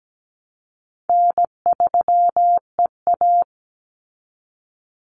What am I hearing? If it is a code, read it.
Morse code, heard as a keyed tone, about 17 wpm, 700 Hz. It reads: N3EA
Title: Easy Call Signs Repeated 2x - 17wpm